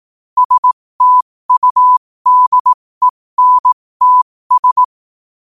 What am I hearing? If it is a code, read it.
Morse code, heard as a keyed tone, about 18 wpm, spaced as for 15 wpm, 1000 Hz. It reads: STUDENTS
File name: qotdfiles / 15wpm